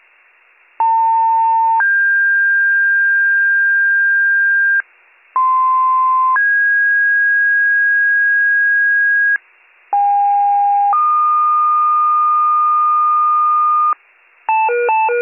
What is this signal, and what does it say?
Two-tone page, heard as a one-second tone, then a three-second tone, about 900 then 1600 Hz, about 1040 then 1650 Hz, about 810 then 1220 Hz, with noise. It is heard from 0.8 to 13.9 s.